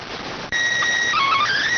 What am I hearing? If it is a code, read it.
SSTV (slow-scan television), PD 290.